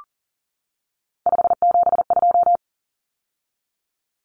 Morse code, heard as a keyed tone, about 40 wpm, 700 Hz. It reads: 572
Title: Common Cut Numbers - 3-Digit 40wpm